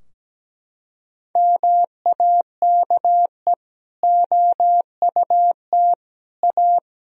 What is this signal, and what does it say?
Morse code, heard as a keyed tone, about 17 wpm, 700 Hz. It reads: MAKE OUT A
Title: Sentences from Top 1000 Words - Review Part 2 - 17wpm